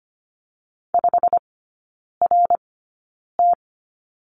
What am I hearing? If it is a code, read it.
Morse code, heard as a keyed tone, about 25 wpm, spaced as for 9 wpm, 690 Hz. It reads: 5RT
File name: Lesson 15e - Number 5 - Three Characters 25wpm